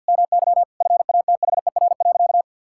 Morse code, heard as a keyed tone, about 51 wpm, 700 Hz. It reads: MY FATHER.
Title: Sentences from Top 1000 Words - Review Part 5 - 50wpm